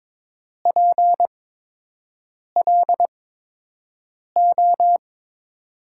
Morse code, heard as a keyed tone, about 22 wpm, spaced as for 6 wpm, 700 Hz. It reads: PLO
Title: USA Call Signs - 8x Character Spacing - 22wpm